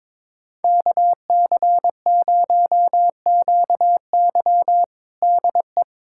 Morse code, heard as a keyed tone, about 21 wpm, 700 Hz. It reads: KC0QY DE
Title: USA Callsign DE Callsign - 5 Characters - 22wpm